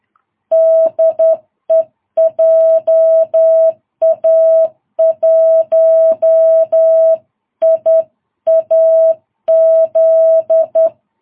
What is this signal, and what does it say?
Morse code, heard as a keyed tone, about 11 wpm, 660 Hz. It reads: DE JA1IAZ
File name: jh8cbh / 2022cw